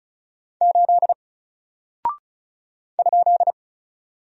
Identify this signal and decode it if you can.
Morse code, heard as a keyed tone, about 36 wpm, 700 Hz. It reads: 8 ?
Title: Single Character (Letter:Number:Punctuation:Prosign) 35wpm